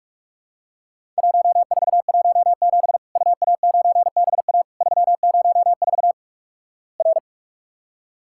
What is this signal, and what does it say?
Morse code, heard as a keyed tone, about 48 wpm, 700 Hz. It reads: 1417 UA9BA 304 R
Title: Worked All Europe DX Contest - QTC Exchange - 45wpm